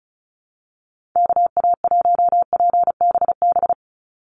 Morse code, heard as a keyed tone, about 34 wpm, 700 Hz. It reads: KA1PBB
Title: USA Call Signs 35wpm